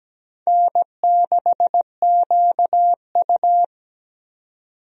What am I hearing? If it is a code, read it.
Morse code, heard as a keyed tone, about 17 wpm, 700 Hz. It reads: N6QU
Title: USA Call Signs - 4 Characters - 17wpm